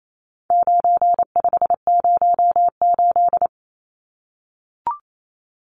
Morse code, heard as a keyed tone, about 28 wpm, 700 Hz. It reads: 9508
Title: Numbers - 4-Digit 28wpm